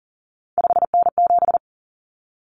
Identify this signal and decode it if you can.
Morse code, heard as a keyed tone, about 40 wpm, 700 Hz. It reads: 5N7
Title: Common Cut Numbers - 3-Digit 40wpm